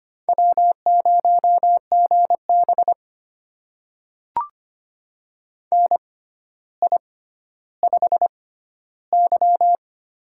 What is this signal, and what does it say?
Morse code, heard as a keyed tone, about 25 wpm, spaced as for 13 wpm, 700 Hz. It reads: W0GB NI5Y
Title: USA Call Signs - 4 Characters - 6x Character Spacing - 25wpm